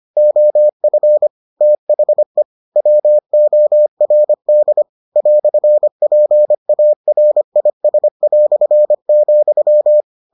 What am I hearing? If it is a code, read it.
Morse code, heard as a keyed tone, about 25 wpm, 600 Hz. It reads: OF THE WORD "PARIS",